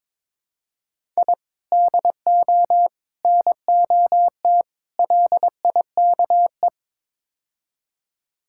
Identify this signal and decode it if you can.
Morse code, heard as a keyed tone, about 22 wpm, 700 Hz. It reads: I DO NOT LIKE
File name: practice / 22wpm / Build A Phrase - 5-Word Phrases - 22wpm